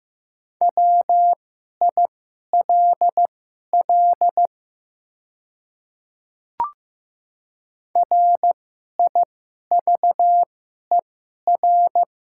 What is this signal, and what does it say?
Morse code, heard as a keyed tone, about 15 wpm, spaced as for 11 wpm, 700 Hz. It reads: WILL RIVER
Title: 2x Character Spacing - Top 300 Words - 15wpm